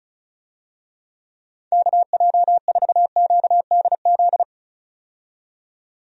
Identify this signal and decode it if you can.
Morse code, heard as a keyed tone, about 36 wpm, 700 Hz. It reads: KJ4QDZ
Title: USA Call Signs - 6 Characters - 35wpm